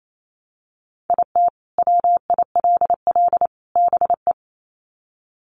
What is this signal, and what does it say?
Morse code, heard as a keyed tone, about 28 wpm, 700 Hz. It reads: IT WILL BE
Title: Sets of 3 Words 28wpm